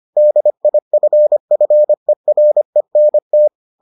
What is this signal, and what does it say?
Morse code, heard as a keyed tone, about 25 wpm, 600 Hz. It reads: DIFFERENT